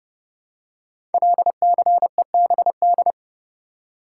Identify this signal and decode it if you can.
Morse code, heard as a keyed tone, about 28 wpm, 700 Hz. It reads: LCEBD
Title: Code Groups - 5 Letters - 30wpm